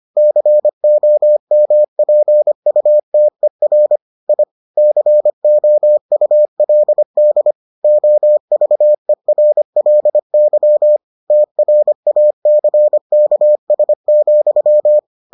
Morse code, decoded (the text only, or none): COMPUTER I COULD OVERLY TRACKS,